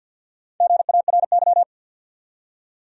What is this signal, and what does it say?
Morse code, heard as a keyed tone, about 50 wpm, 700 Hz. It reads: GARY
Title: USA Common Names - Max 5 Letters - 50wpm